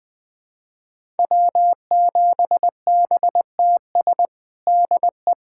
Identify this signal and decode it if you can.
Morse code, heard as a keyed tone, about 20 wpm, 700 Hz. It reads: W7BTS DE
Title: USA Callsign DE Callsign 20wpm